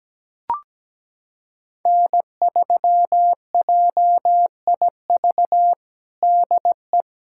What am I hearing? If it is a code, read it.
Morse code, heard as a keyed tone, about 17 wpm, 700 Hz. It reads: N3JIV DE
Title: USA Callsign DE Callsign - 5 Characters - 17wpm